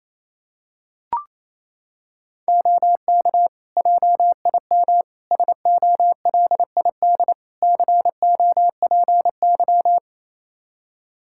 Morse code, heard as a keyed tone, about 28 wpm, 700 Hz. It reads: OK JIM SOLID COPY
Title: Practice QSO - Exchange 3 - More Info - 28wpm